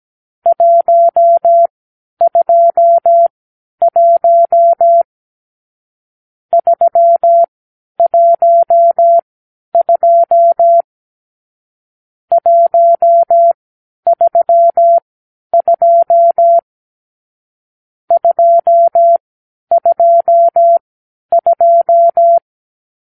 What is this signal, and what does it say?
Morse code, heard as a keyed tone, about 17 wpm, spaced as for 10 wpm, 680 Hz. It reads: 121 312 132 222
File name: k7qo / Lesson 44 (1-3)